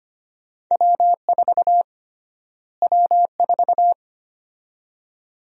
Morse code, heard as a keyed tone, about 25 wpm, 700 Hz. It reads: W4 W4
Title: USA Call Sign Prefixes - Repeated 2x - 25wpm